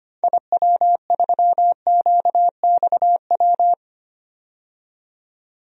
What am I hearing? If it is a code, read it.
Morse code, heard as a keyed tone, about 25 wpm, 700 Hz. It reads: IW3QXW
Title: DXCC-100 Call Signs - 6 Characters - 25wpm